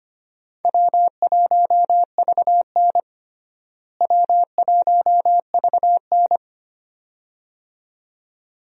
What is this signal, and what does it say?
Morse code, heard as a keyed tone, about 25 wpm, 700 Hz. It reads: W1VN W1VN
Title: USA Call Signs - 4 Characters - Repeated 2x - 25wpm